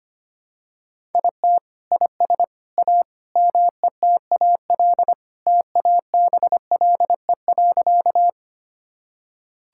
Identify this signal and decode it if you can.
Morse code, heard as a keyed tone, about 25 wpm, 700 Hz. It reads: IT IS A METAL TABLE.
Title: Sentences from Top 1000 Words - Review Part 6 - 25wpm